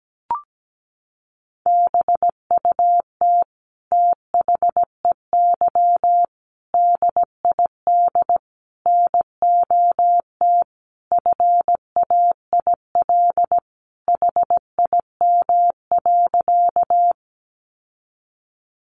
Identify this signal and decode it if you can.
Morse code, heard as a keyed tone, about 17 wpm, 700 Hz. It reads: BUT THEY DID NOT FAIL HIM.
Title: Sentences from Top 1100 Words 17wpm